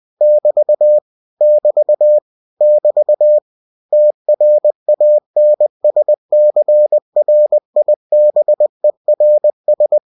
Morse code, heard as a keyed tone, about 20 wpm, 600 Hz. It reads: <BT> <BT> <BT> TRANSCRIBERS
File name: cw-books / winnie-the-pooh / winnie-20wpm-page-0111